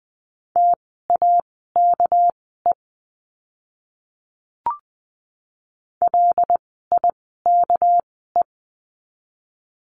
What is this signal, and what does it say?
Morse code, heard as a keyed tone, about 19 wpm, spaced as for 14 wpm, 700 Hz. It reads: TAKE LIKE
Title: 237 Lesson 38f - Letter q - Words 20wpm